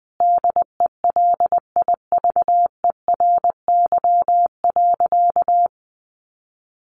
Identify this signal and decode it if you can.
Morse code, heard as a keyed tone, about 20 wpm, 700 Hz. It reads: DELIVERY.
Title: Sentences from Top 1500 Words 20wpm